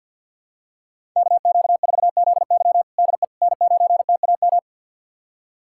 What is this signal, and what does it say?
Morse code, heard as a keyed tone, about 48 wpm, 700 Hz. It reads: KQ4ZY DE N9TAM
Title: USA Callsign DE Callsign - 5 Characters - 50wpm